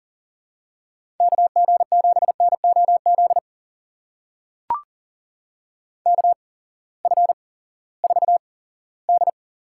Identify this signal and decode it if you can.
Morse code, heard as a keyed tone, about 40 wpm, spaced as for 20 wpm, 700 Hz. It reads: KG7NOZ KF4D
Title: USA Call Signs - 6 Characters - 8x Character Spacing - 40wpm